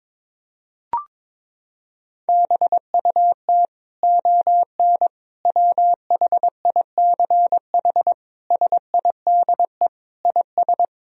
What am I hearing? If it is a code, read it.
Morse code, heard as a keyed tone, about 22 wpm, 700 Hz. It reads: BUT ON WHICH SIDE IS
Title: Sentences from Top 1000 Words - Review Part 1 - 22wpm